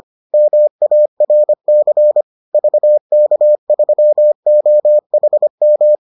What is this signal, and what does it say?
Morse code, heard as a keyed tone, about 25 wpm, 600 Hz. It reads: MARC VK3OHM